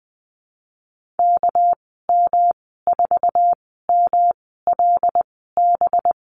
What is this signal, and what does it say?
Morse code, heard as a keyed tone, about 20 wpm, spaced as for 14 wpm, 700 Hz. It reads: KM4MLB
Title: 238 Lesson 38g - Letter q - Call Signs 20wpm